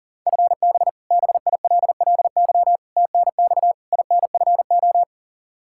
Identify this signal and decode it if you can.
Morse code, heard as a keyed tone, about 37 wpm, 700 Hz. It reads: FB BILLY TNX INFO
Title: K1USN SST QSO Practice 40wpm